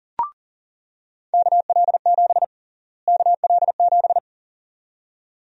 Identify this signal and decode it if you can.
Morse code, heard as a keyed tone, about 40 wpm, 700 Hz. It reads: KL7 KL7
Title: USA Call Sign Prefixes - Repeated 2x - 40wpm